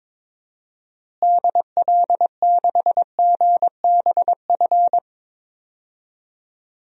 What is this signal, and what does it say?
Morse code, heard as a keyed tone, about 21 wpm, 700 Hz. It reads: DL6GBF
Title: DXCC-100 Call Signs - 6 Characters - Repeated 2x - 22wpm